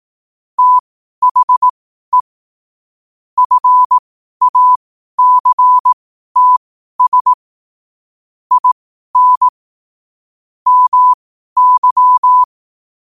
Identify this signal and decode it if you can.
Morse code, heard as a keyed tone, about 18 wpm, spaced as for 12 wpm, 1000 Hz. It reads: THE FACTS IN MY
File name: qotdfiles / 12wpm